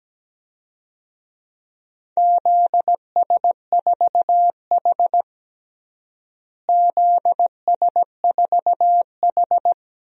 Morse code, heard as a keyed tone, about 17 wpm, 700 Hz. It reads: ZS4H ZS4H
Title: DXCC-100 Call Signs - 4 Characters - Repeated 3x - 17wpm